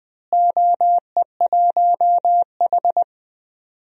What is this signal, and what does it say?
Morse code, heard as a keyed tone, about 20 wpm, 700 Hz. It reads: OE1H
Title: DXCC-100 Call Signs - 4 Characters - Repeated 3x - 20wpm